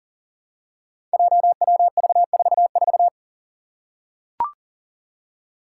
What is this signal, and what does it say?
Morse code, heard as a keyed tone, about 40 wpm, 700 Hz. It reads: JWV44
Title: Five Challenging Letter-Number Combinations 40wpm